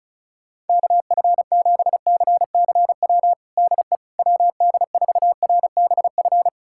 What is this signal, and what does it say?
Morse code, heard as a keyed tone, about 36 wpm, 700 Hz. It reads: KF7CCW DE WD4RBF
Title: USA Callsign DE Callsign - 6 Characters - 35wpm